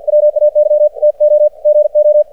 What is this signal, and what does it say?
Morse code, heard as a keyed tone, about 53 wpm, 600 Hz. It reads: WAY TO GO
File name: archived / CRSnet